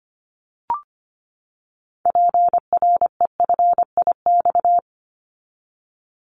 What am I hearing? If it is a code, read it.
Morse code, heard as a keyed tone, about 25 wpm, 700 Hz. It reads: PREFIX
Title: Callsign Prefixes for Top 25 DXCC - 25wpm